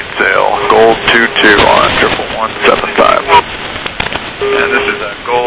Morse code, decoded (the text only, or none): none